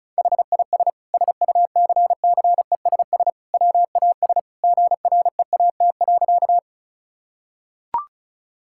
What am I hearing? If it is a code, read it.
Morse code, heard as a keyed tone, about 34 wpm, 690 Hz. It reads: HIS SUCCESS WAS GREAT.
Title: Sentences from Top 1000 Words - Review Part 6 - 35wpm